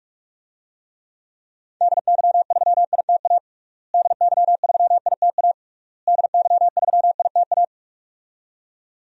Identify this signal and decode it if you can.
Morse code, heard as a keyed tone, about 45 wpm, 700 Hz. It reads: DY3ITA DY3ITA DY3ITA
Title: DXCC-100 Call Signs - 6 Characters - Repeated 3x - 45wpm